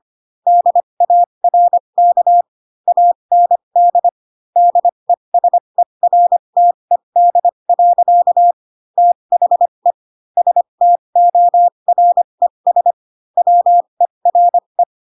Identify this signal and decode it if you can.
Morse code, heard as a keyed tone, about 25 wpm, spaced as for 22 wpm, 700 Hz. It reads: DARK AND DESERTED. THE STORES WERE
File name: Pinocchio / Pinocchio ch6 18 25 ik1hge